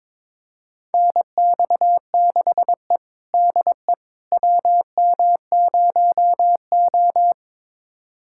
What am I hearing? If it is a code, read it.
Morse code, heard as a keyed tone, about 22 wpm, 700 Hz. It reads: NX6E DE WM0O
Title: USA Callsign DE Callsign - 4 Characters - 22wpm